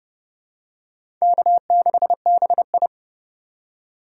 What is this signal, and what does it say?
Morse code, heard as a keyed tone, about 30 wpm, 700 Hz. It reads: K6BI
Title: USA Call Signs - 3x Character Spacing - 30wpm